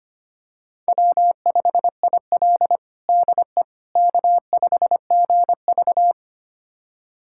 Morse code, heard as a keyed tone, about 25 wpm, 700 Hz. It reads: W5IL DE K5GV
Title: USA Callsign DE Callsign - 4 Characters - 25wpm